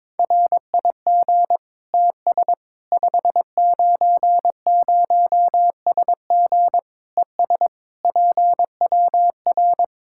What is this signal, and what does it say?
Morse code, heard as a keyed tone, about 21 wpm, 700 Hz. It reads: RIG TS 590SG ES PWR
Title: Practice QSO - Exchange 2 - Info - 22wpm